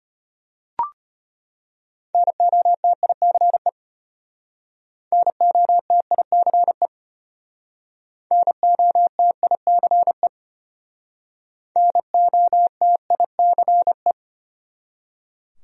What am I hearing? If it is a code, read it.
Morse code, heard as a keyed tone, about 34 wpm, 700 Hz. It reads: NOTICE NOTICE NOTICE NOTICE